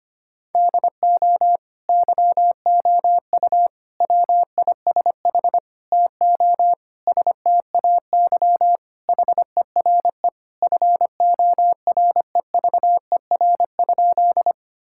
Morse code, heard as a keyed tone, about 25 wpm, 700 Hz. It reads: DO YOU WISH TO STAY HERE FOREVER?
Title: Sentences from Top 1600 Words 25wpm